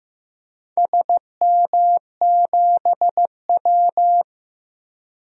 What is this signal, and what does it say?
Morse code, heard as a keyed tone, about 15 wpm, 700 Hz. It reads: SM7W
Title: DXCC-100 Call Signs - 4 Characters - 15wpm